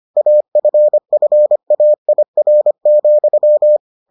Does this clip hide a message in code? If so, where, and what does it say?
Yes: Morse code from 0.2 to 3.8 s, heard as a keyed tone, about 25 wpm, 600 Hz. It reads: AFFAIR,